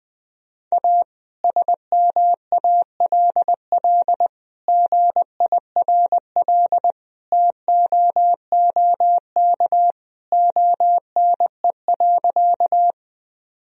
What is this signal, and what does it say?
Morse code, heard as a keyed tone, about 20 wpm, 700 Hz. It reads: A SMALL GIRL TOOK ONE.
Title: Sentences from Top 1000 Words - Review Part 1 - 20wpm